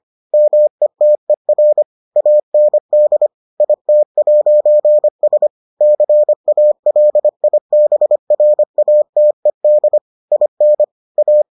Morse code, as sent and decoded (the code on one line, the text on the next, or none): -- . - . .-. / .- -. -.. / .. - .----. ... / -.-. .- .-.. .. -... .-. .- - . -.. / .. -. / .-
METER AND IT'S CALIBRATED IN A